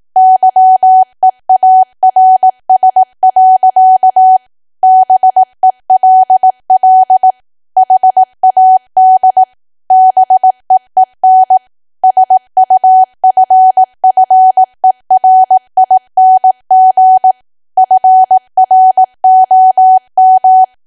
Morse code, heard as a keyed tone, about 18 wpm, 750 Hz. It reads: YEARS. BELL HAD BEEN SUFFERING FROM
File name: Archive / 18 WPM / 18WPM